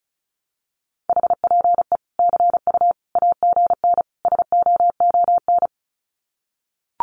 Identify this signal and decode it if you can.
Morse code, heard as a keyed tone, about 34 wpm, 700 Hz. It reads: HPE CU AGN SOON
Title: Common QSO Phrases 35wpm